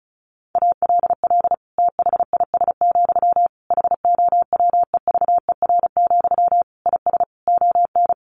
Morse code, heard as a keyed tone, about 34 wpm, 700 Hz. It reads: ALL THIS, HOWEVER, IS ON
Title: Sentences from Top 600 Words 35wpm